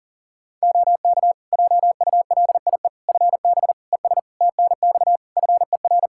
Morse code, heard as a keyed tone, about 37 wpm, 700 Hz. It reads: OK JULIE FB ES TNX FER
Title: Practice QSO - Exchange 2 - Info - 40wpm